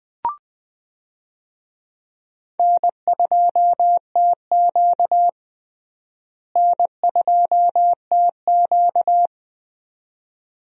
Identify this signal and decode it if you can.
Morse code, heard as a keyed tone, about 20 wpm, 700 Hz. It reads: N2TQ N2TQ
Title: USA Call Signs - 4 Characters - Repeated 2x - 20wpm